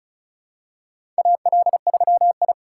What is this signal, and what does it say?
Morse code, heard as a keyed tone, about 36 wpm, 700 Hz. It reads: AL3I
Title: USA Call Signs - 4 Characters - 4x Character Spacing - 35wpm